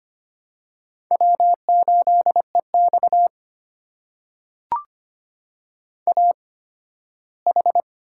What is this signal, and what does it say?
Morse code, heard as a keyed tone, about 25 wpm, spaced as for 15 wpm, 700 Hz. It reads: W8EX AH
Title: USA Call Signs - 4 Characters - 8x Character Spacing - 25wpm